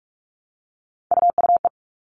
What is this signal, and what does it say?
Morse code, heard as a keyed tone, about 44 wpm, 700 Hz. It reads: UUE
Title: All Cut Numbers - 3-Digit 45wpm